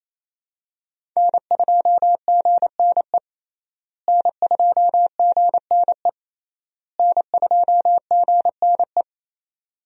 Morse code, heard as a keyed tone, about 28 wpm, 700 Hz. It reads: N2GNE N2GNE N2GNE
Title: USA Call Signs - 5 Characters - Repeated 3x - 28wpm